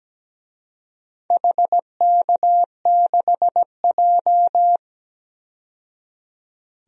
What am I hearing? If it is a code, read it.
Morse code, heard as a keyed tone, about 17 wpm, 700 Hz. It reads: HK6J
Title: DXCC-100 Call Signs - 4 Characters - 17wpm